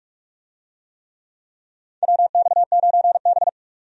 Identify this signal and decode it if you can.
Morse code, heard as a keyed tone, about 41 wpm, 700 Hz. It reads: WX9B
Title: USA Call Signs - 4 Characters - 45wpm